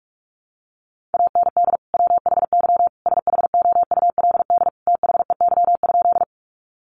Morse code, heard as a keyed tone, about 44 wpm, 700 Hz. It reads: AND WHY SHOULD THEY?
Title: Sentences from Top 300 Words 45wpm